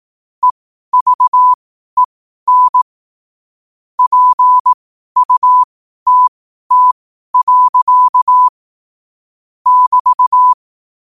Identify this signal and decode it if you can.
Morse code, heard as a keyed tone, about 18 wpm, spaced as for 12 wpm, 1000 Hz. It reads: EVEN PUTT. <BT>
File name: qotdfiles / 12wpm